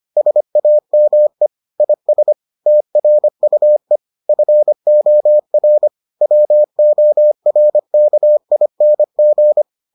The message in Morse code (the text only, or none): SAME IS TRUE FOR WORKING